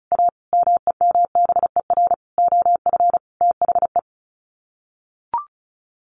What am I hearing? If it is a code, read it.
Morse code, heard as a keyed tone, about 34 wpm, 700 Hz. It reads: A MEMBER OF THE
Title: Sets of 5 Words 35wpm